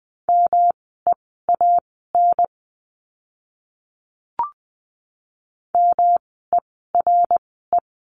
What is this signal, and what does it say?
Morse code, heard as a keyed tone, about 20 wpm, spaced as for 15 wpm, 700 Hz. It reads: MEAN MERE
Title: 108 Lesson 18f - Letter m - Words 20wpm